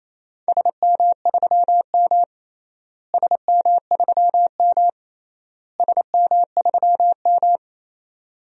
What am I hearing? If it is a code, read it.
Morse code, heard as a keyed tone, about 27 wpm, 700 Hz. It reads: SM3M SM3M SM3M
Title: DXCC-100 Call Signs - 4 Characters - Repeated 3x - 28wpm